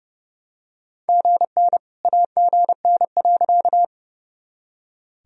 Morse code, heard as a keyed tone, about 31 wpm, 700 Hz. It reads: GN AGN.
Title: Sentences from Top 300 Words and Abbreviations 30wpm